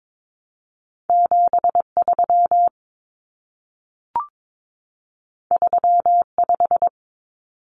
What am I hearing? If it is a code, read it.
Morse code, heard as a keyed tone, about 22 wpm, 700 Hz. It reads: 73 35
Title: Numbers - 2-Digit 22wpm